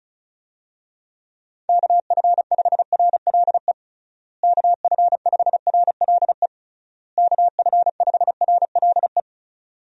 Morse code, heard as a keyed tone, about 34 wpm, 700 Hz. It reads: KF5RLE KF5RLE KF5RLE
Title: USA Call Signs - 6 Characters - Repeated 3x - 35wpm